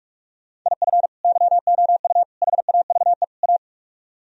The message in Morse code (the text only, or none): IF YOU HAVE A